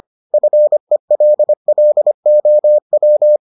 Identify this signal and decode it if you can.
Morse code, heard as a keyed tone, about 25 wpm, 600 Hz. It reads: FELLOW